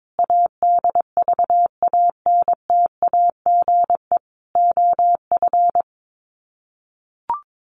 Morse code, heard as a keyed tone, about 22 wpm, 700 Hz. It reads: ADVANTAGE OF
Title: Sets of 4 Words 22wpm